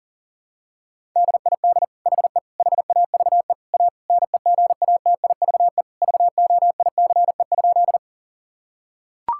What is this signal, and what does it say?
Morse code, heard as a keyed tone, about 40 wpm, 700 Hz. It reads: DID HE HAVE A NEGATIVE VOICE?
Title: Sentences from Top 1000 Words - Review Part 6 - 40wpm